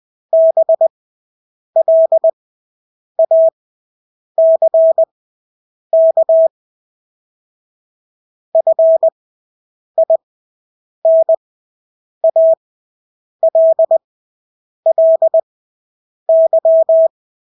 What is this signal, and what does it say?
Morse code, heard as a keyed tone, about 20 wpm, spaced as for 8 wpm, 650 Hz. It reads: BLACK FINALLY
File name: Engelske ord 20/8 wpm